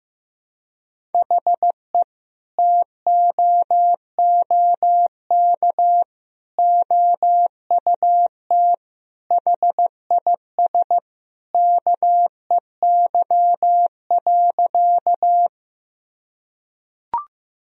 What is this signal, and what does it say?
Morse code, heard as a keyed tone, about 16 wpm, 700 Hz. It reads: HE TOOK OUT HIS KEY.
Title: Sentences from Top 1000 Words - Review Part 4 - 15wpm